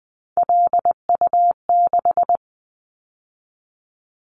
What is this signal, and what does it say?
Morse code, heard as a keyed tone, about 20 wpm, 700 Hz. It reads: LU6
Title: Callsign Prefixes for Top 50 DXCC - 20wpm